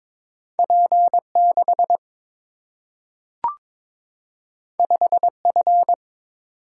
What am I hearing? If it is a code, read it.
Morse code, heard as a keyed tone, about 22 wpm, 700 Hz. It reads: P 6 5 F
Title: Single-Letter Single-Number 22wpm